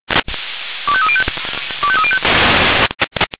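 Five-tone selective call (ZVEI), heard as quick runs of five tones, tones of about 70 ms. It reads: 25305 25305
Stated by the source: ZVEI five-tone selective call 25220 received on an SDR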